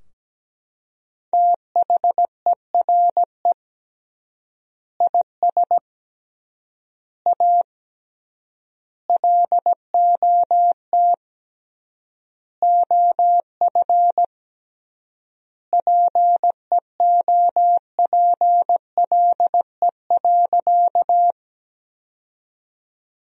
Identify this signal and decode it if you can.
Morse code, heard as a keyed tone, about 17 wpm, 700 Hz. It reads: THERE IS A LOT OF PEOPLE.
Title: 3x Word-Spacing - Sentences from Top 500 Words - 17wpm